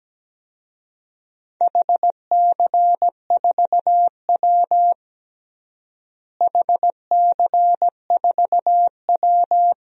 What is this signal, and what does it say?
Morse code, heard as a keyed tone, about 17 wpm, 700 Hz. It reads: HC4W HC4W
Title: DXCC-100 Call Signs - 4 Characters - Repeated 3x - 17wpm